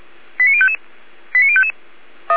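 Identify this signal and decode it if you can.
Five-tone selective call (ZVEI), heard as quick runs of five tones, tones of about 70 ms. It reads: 89055 89055
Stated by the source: ZVEI five-tone selective call 89055 received on an SDR.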